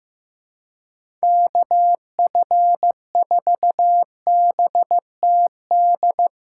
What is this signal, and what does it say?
Morse code, heard as a keyed tone, about 15 wpm, 700 Hz. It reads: KF4BTD
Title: USA Call Signs - 6 Characters - 3x Character Spacing - 15wpm